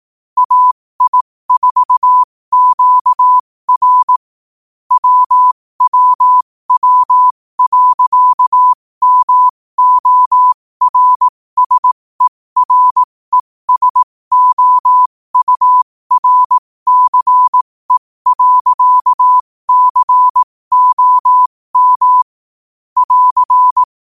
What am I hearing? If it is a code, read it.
Morse code, heard as a keyed tone, about 18 wpm, spaced as for 15 wpm, 1000 Hz. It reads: AI4QR WWW.MORSERESOURCE.COM <AR>
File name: qotdfiles / 15wpm